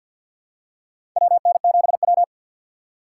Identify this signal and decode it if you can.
Morse code, heard as a keyed tone, about 52 wpm, 700 Hz. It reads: WN7W
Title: USA Call Signs - 4 Characters - 2x Character Spacing - 50wpm